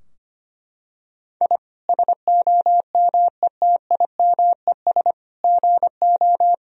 Morse code, heard as a keyed tone, about 25 wpm, 700 Hz. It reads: I SOMETIMES GO
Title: Sentences from Top 1000 Words - Review Part 3 - 25wpm